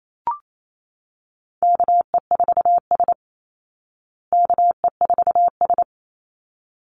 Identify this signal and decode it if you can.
Morse code, heard as a keyed tone, about 28 wpm, 700 Hz. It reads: KE4S KE4S
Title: Call Signs Repeated 3x - 28wpm